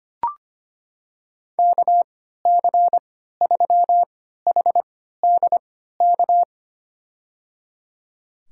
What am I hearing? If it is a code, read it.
Morse code, heard as a keyed tone, about 25 wpm, spaced as for 14 wpm, 700 Hz. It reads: KC3HDK
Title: USA Call Signs - 6 Characters - 3x Character Spacing - 25wpm